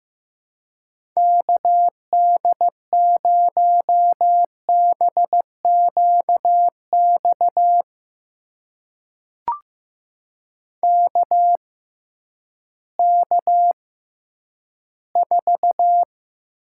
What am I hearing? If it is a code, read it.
Morse code, heard as a keyed tone, about 16 wpm, spaced as for 10 wpm, 700 Hz. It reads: KD0BQX KK4
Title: USA Call Signs - 6 Characters - 6x Character Spacing - 15wpm